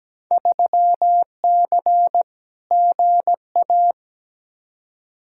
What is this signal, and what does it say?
Morse code, heard as a keyed tone, about 17 wpm, 700 Hz. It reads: 3C GA
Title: Field Day Exchanges Encore - 17wpm